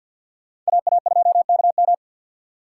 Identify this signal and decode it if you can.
Morse code, heard as a keyed tone, about 62 wpm, 700 Hz. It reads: AA2KM
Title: Speed Racing - USA Call Signs 50wpm